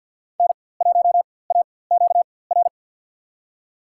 Morse code, heard as a keyed tone, about 47 wpm, spaced as for 23 wpm, 700 Hz. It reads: N1AQR
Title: USA Call Signs - 5 Characters - 4x Character Spacing - 50wpm